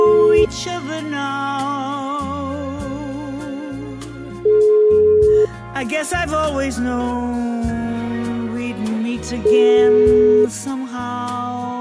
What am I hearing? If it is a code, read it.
Ringback tone (Europe, 425 Hz), with noise.